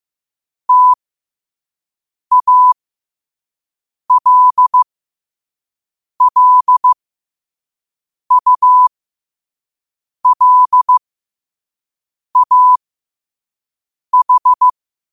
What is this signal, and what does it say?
Morse code, heard as a keyed tone, about 15 wpm, spaced as for 5 wpm, 1000 Hz. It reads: TALLULAH
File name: 5wpm